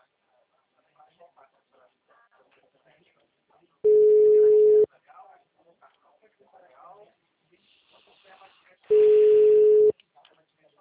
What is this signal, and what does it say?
Ringback tone (Europe, 425 Hz), with speech.
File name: out- -2018- - - .38452